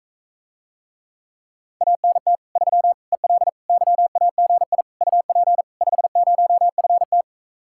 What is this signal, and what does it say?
Morse code, heard as a keyed tone, about 45 wpm, 700 Hz. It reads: ANT 3 EL YAGI UP 50FT